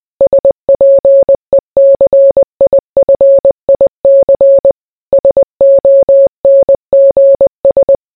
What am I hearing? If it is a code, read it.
Morse code, heard as a keyed tone, about 20 wpm, 550 Hz. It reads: SPECIFIC SONGS